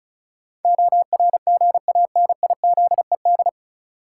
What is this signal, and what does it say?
Morse code, heard as a keyed tone, about 36 wpm, 700 Hz. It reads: ORGANIZED
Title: Sentences from Top 1300 Words - New Word Drill - 35wpm